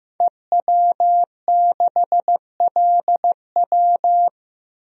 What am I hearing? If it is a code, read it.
Morse code, heard as a keyed tone, about 15 wpm, 700 Hz. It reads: EW6LW
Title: DXCC-100 Call Signs - 5 Characters - 15wpm